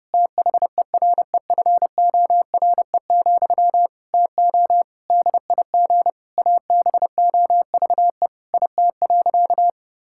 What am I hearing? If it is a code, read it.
Morse code, heard as a keyed tone, about 28 wpm, 700 Hz. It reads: THEREFORE, TO DIG ABOVE IT.
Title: Sentences from Top 1600 Words 30wpm